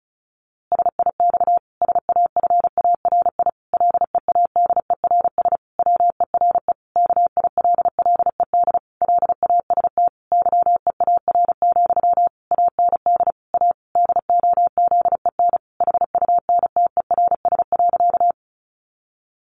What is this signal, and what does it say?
Morse code, heard as a keyed tone, about 34 wpm, 700 Hz. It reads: SIX SAFARI LEADERS WERE KILLED LAST YEAR, AND A DOZEN HUNTERS.
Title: A Sound Of Thunder 35wpm